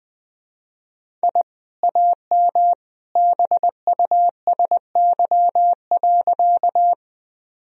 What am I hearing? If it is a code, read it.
Morse code, heard as a keyed tone, about 20 wpm, 700 Hz. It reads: I AM BUSY.
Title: Sentences from Top 1000 Words - Review Part 3 - 20wpm